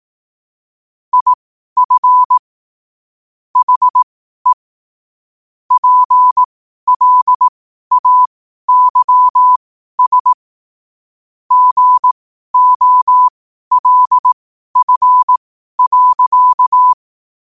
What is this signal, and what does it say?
Morse code, heard as a keyed tone, about 18 wpm, spaced as for 12 wpm, 1000 Hz. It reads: IF HE PLAYS GOLF.
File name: qotdfiles / 12wpm